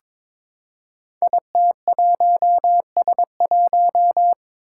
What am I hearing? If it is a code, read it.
Morse code, heard as a keyed tone, about 22 wpm, 700 Hz. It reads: IT1S1
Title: Five Challenging Letter-Number Combinations 22wpm